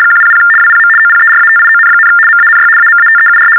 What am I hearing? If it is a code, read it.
RTTY, about 75 baud. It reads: 100 WPM OPERATION, MANY AMATEUR
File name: teletype-RTTY-75